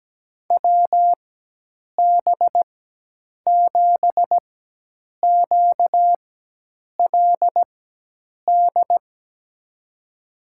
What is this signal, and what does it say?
Morse code, heard as a keyed tone, about 17 wpm, spaced as for 8 wpm, 700 Hz. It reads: WB7QLD
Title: USA Call Signs - 4x Character Spacing - 17wpm